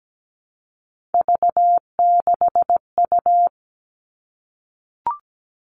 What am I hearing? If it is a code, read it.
Morse code, heard as a keyed tone, about 17 wpm, 700 Hz. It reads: V6U
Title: Three Challenging Letter-Number Combinations 17wpm